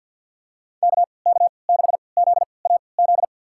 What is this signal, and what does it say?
Morse code, heard as a keyed tone, about 52 wpm, spaced as for 28 wpm, 700 Hz. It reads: KK6ZAZ
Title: USA Call Signs - 3x Character Spacing - 50wpm